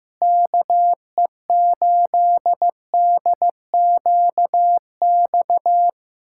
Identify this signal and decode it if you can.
Morse code, heard as a keyed tone, about 15 wpm, 700 Hz. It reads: KE8DQX
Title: USA Callsign DE Callsign 15wpm